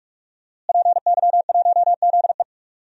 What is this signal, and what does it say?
Morse code, heard as a keyed tone, about 46 wpm, 700 Hz. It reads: PY1ZE
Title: DXCC-100 Call Signs - 5 Characters - Repeated 3x - 45wpm